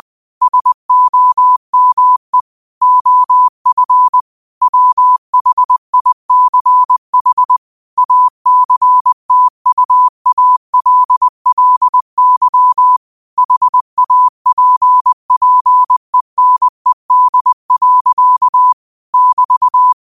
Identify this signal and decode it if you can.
Morse code, heard as a keyed tone, about 20 wpm, 1000 Hz. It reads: SOME OF WHICH ACTUALLY HAPPENED. <BT>